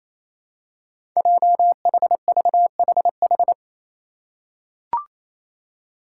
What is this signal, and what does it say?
Morse code, heard as a keyed tone, about 28 wpm, 700 Hz. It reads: JHVHH
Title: Five Challenging Letter-Number Combinations 28wpm